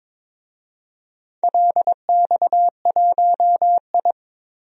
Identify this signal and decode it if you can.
Morse code, heard as a keyed tone, about 22 wpm, 700 Hz. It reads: LX1I
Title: DXCC-100 Call Signs - 4 Characters - 22wpm